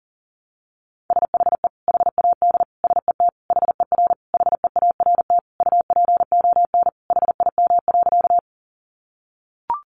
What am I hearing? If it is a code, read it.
Morse code, heard as a keyed tone, about 40 wpm, 700 Hz. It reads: SHE HAD SET HER HEART UPON HIM.
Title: Sentences from Top 1200 Words 40wpm